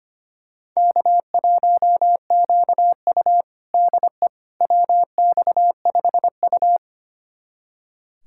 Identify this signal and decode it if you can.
Morse code, heard as a keyed tone, about 25 wpm, 700 Hz. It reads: K1QU DE WX5U
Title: USA Callsign DE Callsign - 4 Characters - 25wpm